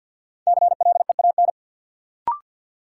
Morse code, heard as a keyed tone, about 51 wpm, 700 Hz. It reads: CLEAN
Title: Sentences from Top 1100 Words - New Word Drill Alt - 50wpm